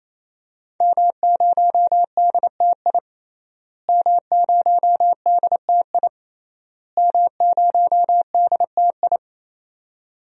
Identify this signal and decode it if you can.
Morse code, heard as a keyed tone, about 28 wpm, 700 Hz. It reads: M0DTI M0DTI M0DTI
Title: DXCC-100 Call Signs - 5 Characters - Repeated 3x - 28wpm